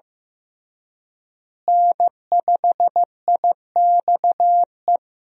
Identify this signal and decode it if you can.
Morse code, heard as a keyed tone, about 15 wpm, 700 Hz. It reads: N5IXE
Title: USA Call Signs - 5 Characters - Repeated 3x - 15wpm